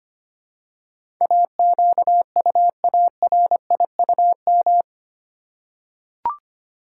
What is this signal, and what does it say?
Morse code, heard as a keyed tone, about 25 wpm, 700 Hz. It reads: AQUARIUM
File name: practice / 25wpm / Easy Words with Hints - 25wpm